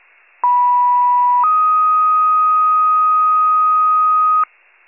Two-tone page, heard as a one-second tone, then a three-second tone, about 980 then 1250 Hz. It is heard from 0.4 to 4.4 s.